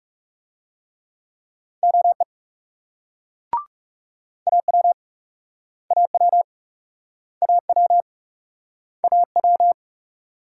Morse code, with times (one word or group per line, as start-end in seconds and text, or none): 1.8-2.2 OE
4.5-4.9 AW
5.9-6.4 AW
7.4-8.0 AW
9.0-9.7 AW